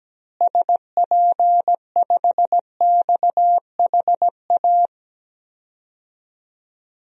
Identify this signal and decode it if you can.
Morse code, heard as a keyed tone, about 17 wpm, 700 Hz. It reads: SP5XHA
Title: DXCC-100 Call Signs - 6 Characters - Repeated 2x - 17wpm